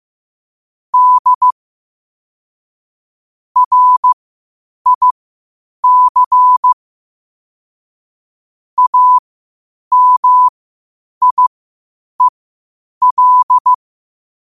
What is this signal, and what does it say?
Morse code, heard as a keyed tone, about 15 wpm, spaced as for 8 wpm, 1000 Hz. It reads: D RIC AMIEL
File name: qotdfiles / 8wpm